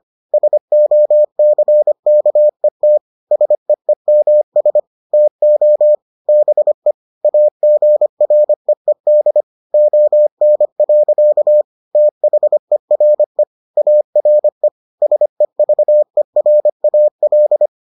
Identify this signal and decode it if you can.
Morse code, heard as a keyed tone, about 25 wpm, 600 Hz. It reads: SOCKET SEEMS TO BE AGREED ON. THERE ARE SEVERAL